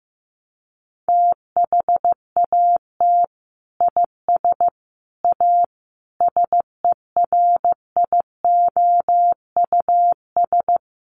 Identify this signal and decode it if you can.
Morse code, heard as a keyed tone, about 15 wpm, 700 Hz. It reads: THAT IS A SERIOUS
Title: Sentences from Top 1100 Words 15wpm